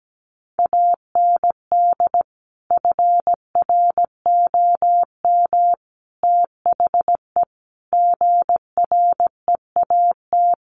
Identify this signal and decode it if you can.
Morse code, heard as a keyed tone, about 17 wpm, 700 Hz. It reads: AND FROM THE GREAT
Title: A Sound Of Thunder 17wpm